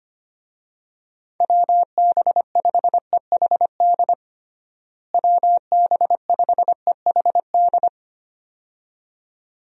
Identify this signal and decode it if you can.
Morse code, heard as a keyed tone, about 25 wpm, 700 Hz. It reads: WB5EHD WB5EHD
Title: USA Call Signs - 6 Characters - Repeated 2x - 25wpm